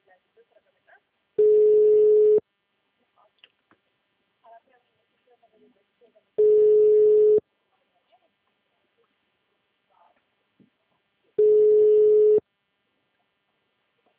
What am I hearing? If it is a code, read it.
Ringback tone (Europe, 425 Hz), with speech.